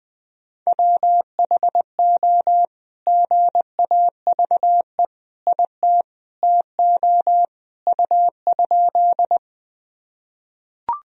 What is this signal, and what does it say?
Morse code, heard as a keyed tone, about 20 wpm, 700 Hz. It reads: WHO GAVE IT TO U?
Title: Sentences from Top 500 Words and Abbreviations 20wpm